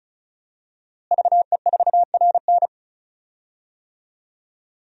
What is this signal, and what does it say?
Morse code, heard as a keyed tone, about 34 wpm, 700 Hz. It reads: VE4RN
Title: DXCC-100 Call Signs - 5 Characters - Repeated 3x - 35wpm